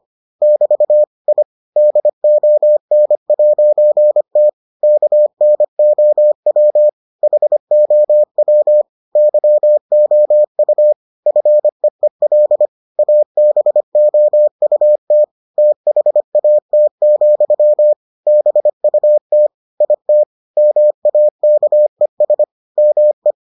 Morse code, heard as a keyed tone, about 25 wpm, 600 Hz. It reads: <BT> I DON'T KNOW HOW YOU FEEL ABOUT THAT, BUT IT MAKES ME